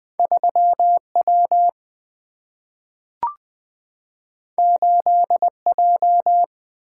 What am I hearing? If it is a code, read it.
Morse code, heard as a keyed tone, about 20 wpm, 700 Hz. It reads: 3 W 8 J
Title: Single-Letter Single-Number 20wpm